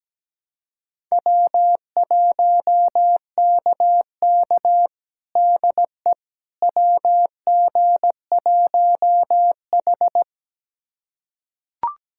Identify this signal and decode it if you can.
Morse code, heard as a keyed tone, about 17 wpm, 700 Hz. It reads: W1KK DE WG1H
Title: USA Callsign DE Callsign - 4 Characters - 17wpm